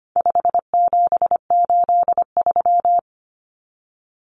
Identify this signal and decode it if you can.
Morse code, heard as a keyed tone, about 25 wpm, 700 Hz. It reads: 5783
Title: Numbers - 4-Digit 25wpm